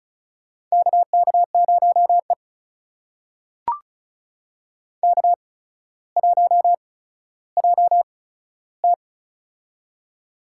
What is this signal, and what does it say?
Morse code, heard as a keyed tone, about 34 wpm, spaced as for 15 wpm, 700 Hz. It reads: KK0E K1JT
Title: USA Call Signs - 4 Characters - 8x Character Spacing - 35wpm